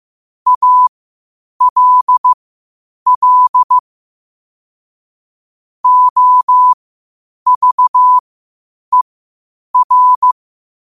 Morse code, heard as a keyed tone, about 15 wpm, spaced as for 8 wpm, 1000 Hz. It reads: ALL OVER
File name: qotdfiles / 8wpm